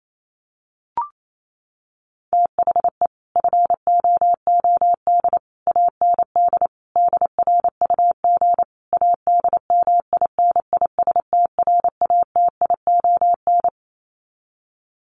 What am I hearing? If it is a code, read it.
Morse code, heard as a keyed tone, about 28 wpm, 700 Hz. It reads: THE FOOD AND DRUG ADMINISTRATION
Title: Sets of 5 Words 28wpm